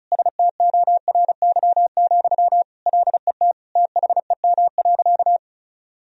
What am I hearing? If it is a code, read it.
Morse code, heard as a keyed tone, about 36 wpm, 700 Hz. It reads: STORY, LET THEM.
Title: Sentences from Top 1000 Words - Review Part 2 - 35wpm